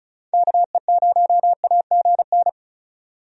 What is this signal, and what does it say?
Morse code, heard as a keyed tone, about 34 wpm, 700 Hz. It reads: KE0AGN
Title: USA Call Signs - 6 Characters - 35wpm